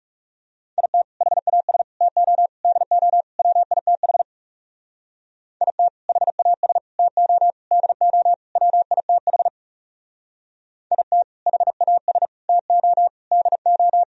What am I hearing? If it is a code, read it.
Morse code, heard as a keyed tone, about 36 wpm, 700 Hz. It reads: IT HAS TO DO WITH IT HAS TO DO WITH IT HAS TO DO
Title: Speed-Racing - Sets of 5 Words 30wpm